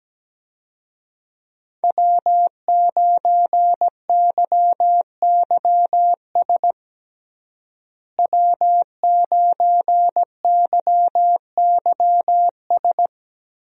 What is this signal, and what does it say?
Morse code, heard as a keyed tone, about 17 wpm, 700 Hz. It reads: W9YYS W9YYS
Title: USA Call Signs - 5 Characters - Repeated 2x - 17wpm